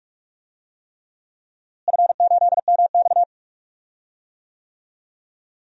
Morse code, heard as a keyed tone, about 41 wpm, 700 Hz. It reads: F8MX
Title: DXCC-100 Call Signs - 4 Characters - Repeated 3x - 45wpm